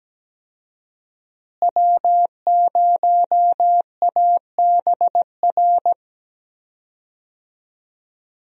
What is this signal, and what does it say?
Morse code, heard as a keyed tone, about 17 wpm, 700 Hz. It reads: W0ABR
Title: USA Call Signs - 5 Characters - Repeated 3x - 17wpm